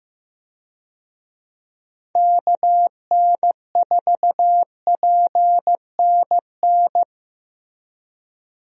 Morse code, heard as a keyed tone, about 15 wpm, 700 Hz. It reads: KN4PNN
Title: USA Call Signs - 6 Characters - Repeated 3x - 15wpm